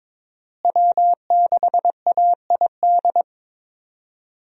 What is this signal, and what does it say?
Morse code, heard as a keyed tone, about 23 wpm, 700 Hz. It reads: W6AID
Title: USA Call Signs - 5 Characters - Repeated 3x - 22wpm